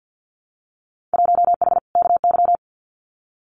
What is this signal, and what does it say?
Morse code, heard as a keyed tone, about 50 wpm, 700 Hz. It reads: 1H KY
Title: Winter Field Day Exchanges 50wpm